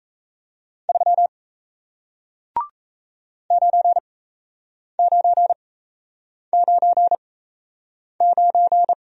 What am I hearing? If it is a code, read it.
Morse code, heard as a keyed tone, about 37 wpm, spaced as for 8 wpm, 700 Hz. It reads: 3 9999